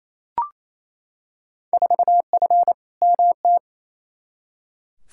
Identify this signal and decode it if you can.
Morse code, heard as a keyed tone, about 28 wpm, 700 Hz. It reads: 4F MT
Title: Field Day Exchanges Encore - 28wpm